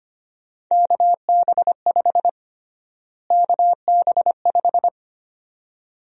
Morse code, heard as a keyed tone, about 25 wpm, 700 Hz. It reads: KB5 KB5
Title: USA Call Sign Prefixes - Repeated 2x - 25wpm